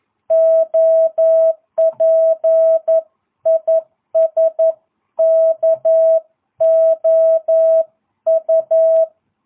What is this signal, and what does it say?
Morse code, heard as a keyed tone, about 11 wpm, 650 Hz. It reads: OP IS KOU